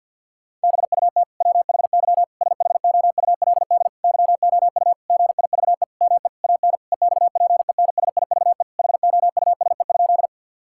Morse code, heard as a keyed tone, about 50 wpm, 700 Hz. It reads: BUT WHY SHOULD YOU GIVE ME AN EXPENSIVE HOUSE?
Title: Sentences from Top 1000 Words - Review Part 6 - 50wpm